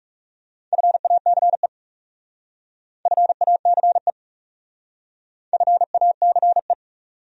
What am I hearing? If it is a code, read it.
Morse code, heard as a keyed tone, about 39 wpm, 700 Hz. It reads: FACE FACE FACE